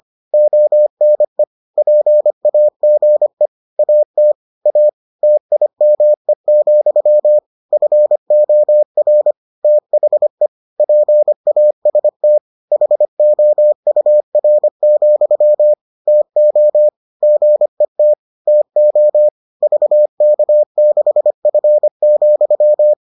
Morse code, heard as a keyed tone, about 25 wpm, 600 Hz. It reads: ONE PAGE AT A TIME, FOR THE PAST HOUR, TO GET TO VK6F,